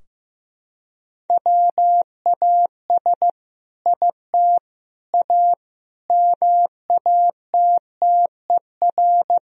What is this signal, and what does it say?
Morse code, heard as a keyed tone, about 15 wpm, 700 Hz. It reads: WAS IT A MATTER
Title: Sentences from Top 1000 Words - Review Part 4 - 15wpm